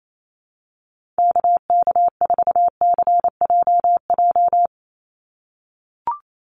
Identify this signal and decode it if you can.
Morse code, heard as a keyed tone, about 28 wpm, 700 Hz. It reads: KK4CJJ
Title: USA Call Signs 28wpm